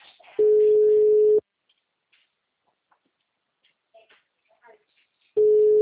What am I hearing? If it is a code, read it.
Ringback tone (Europe, 425 Hz).
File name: out- -2019- - - .593